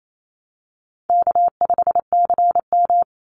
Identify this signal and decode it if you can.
Morse code, heard as a keyed tone, about 28 wpm, 700 Hz. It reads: K5CM
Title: Call Signs Repeated 2x - 28wpm